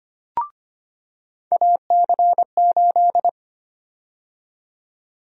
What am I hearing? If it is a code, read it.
Morse code, heard as a keyed tone, about 25 wpm, 700 Hz. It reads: AC8
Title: USA Call Sign Prefixes - 25wpm